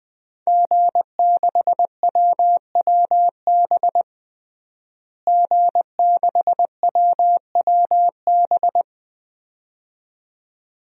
Morse code, heard as a keyed tone, about 20 wpm, 700 Hz. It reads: G6WWB G6WWB
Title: DXCC-100 Call Signs - 5 Characters - Repeated 2x - 20wpm